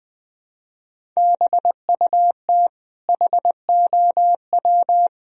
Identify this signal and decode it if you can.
Morse code, heard as a keyed tone, about 20 wpm, 700 Hz. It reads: BUT HOW
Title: Sentences from Top 1000 Words - Review Part 1 - 20wpm